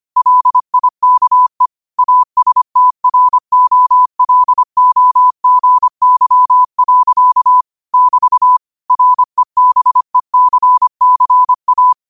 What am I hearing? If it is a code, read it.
Morse code, heard as a keyed tone, about 25 wpm, 1000 Hz. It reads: LIKE ASTROLOGY. <BT> REBECCA